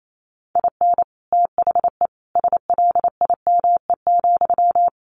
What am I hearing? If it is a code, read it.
Morse code, heard as a keyed tone, about 28 wpm, 700 Hz. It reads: IN THE SLIME,
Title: A Sound Of Thunder 28wpm